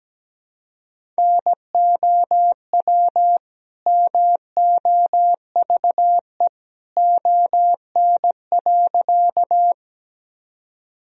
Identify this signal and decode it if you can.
Morse code, heard as a keyed tone, about 17 wpm, 700 Hz. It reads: NOW MOVE ON.
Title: Sentences from Top 1000 Words - Review Part 1 - 17wpm